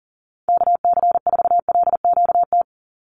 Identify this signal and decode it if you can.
Morse code, heard as a keyed tone, about 40 wpm, 700 Hz. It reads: KC4LQT
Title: USA Call Signs 40wpm